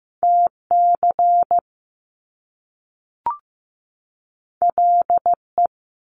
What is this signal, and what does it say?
Morse code, heard as a keyed tone, about 15 wpm, 700 Hz. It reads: TC LE
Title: Top Two Letter Combinations in Words 15wpm